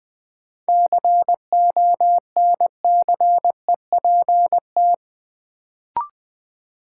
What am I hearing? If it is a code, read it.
Morse code, heard as a keyed tone, about 21 wpm, 700 Hz. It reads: CONCEPT
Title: Sentences from Top 1100 Words - New Word Drill - 20wpm